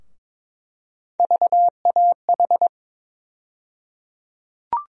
Morse code, heard as a keyed tone, about 22 wpm, 700 Hz. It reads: VAH
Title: USA Call Sign Postfixes - 3 Letters 22wpm